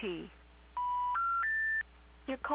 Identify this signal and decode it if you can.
Special information tone, with speech.